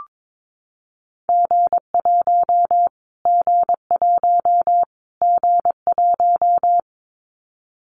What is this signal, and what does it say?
Morse code, heard as a keyed tone, about 22 wpm, 700 Hz. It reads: G1 G1 G1
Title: Callsign Prefixes for Top 25 DXCC - Repeated 3x - 22wpm